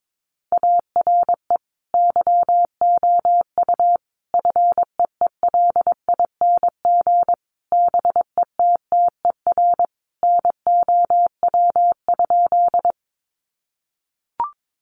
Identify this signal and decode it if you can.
Morse code, heard as a keyed tone, about 22 wpm, 700 Hz. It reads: ARE YOU FEELING BETTER NOW?
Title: Sentences from Top 1200 Words 22wpm